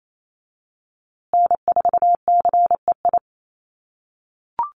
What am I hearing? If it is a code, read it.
Morse code, heard as a keyed tone, about 28 wpm, 700 Hz. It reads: N4CEI
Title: USA Call Signs Repeated 2x 28wpm